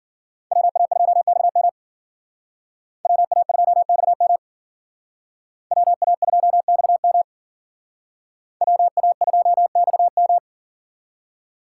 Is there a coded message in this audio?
Yes: Morse code from 0.5 to 10.4 s, heard as a keyed tone, about 51 wpm, 700 Hz. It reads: WA2XM WA2XM WA2XM WA2XM